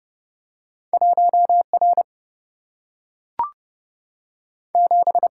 Morse code, heard as a keyed tone, about 30 wpm, 700 Hz. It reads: 1 R 7
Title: Single-Letter Single-Number 30wpm